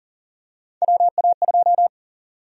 Morse code, heard as a keyed tone, about 40 wpm, 700 Hz. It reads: WA2
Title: USA Call Sign Prefixes - 40wpm